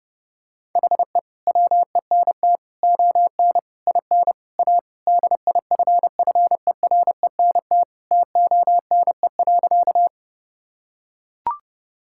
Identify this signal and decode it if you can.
Morse code, heard as a keyed tone, about 32 wpm, 700 Hz. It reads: HE WENT ON IN A DIFFERENT TONE.
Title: Sentences from Top 1600 Words 30wpm